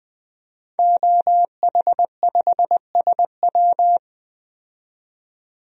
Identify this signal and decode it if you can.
Morse code, heard as a keyed tone, about 20 wpm, 700 Hz. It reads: OH5SW
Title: DXCC-100 Call Signs - 5 Characters - 20wpm